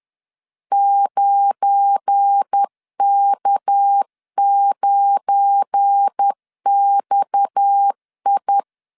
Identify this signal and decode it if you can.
Morse code, heard as a keyed tone, about 10 wpm, 800 Hz. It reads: 9K9XI
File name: LEZIONI / 7 GF9Z / 7 GF9 10wpm 2747